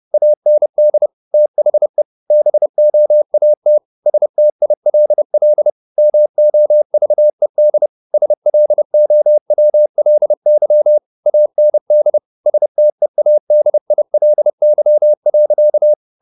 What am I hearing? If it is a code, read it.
Morse code, heard as a keyed tone, about 30 wpm, 600 Hz. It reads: AND THE BOAT STILL MOVED SLOWLY AND STEADILY.
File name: the old man and the sea / the old man and the sea 1 30wpm